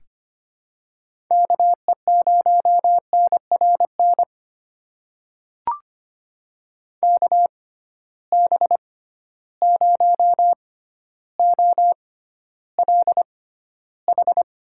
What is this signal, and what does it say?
Morse code, heard as a keyed tone, about 25 wpm, spaced as for 13 wpm, 700 Hz. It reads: KE0NRN KB0OLH
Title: USA Call Signs - 6 Characters - 6x Character Spacing - 25wpm